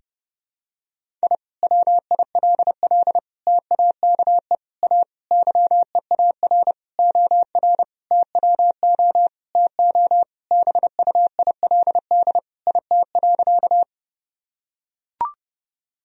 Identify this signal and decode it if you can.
Morse code, heard as a keyed tone, about 32 wpm, 700 Hz. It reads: I WILL TAKE A YEAR OR TWO TO BUILD IT.
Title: Sentences from Top 1000 Words - Review Part 1 - 30wpm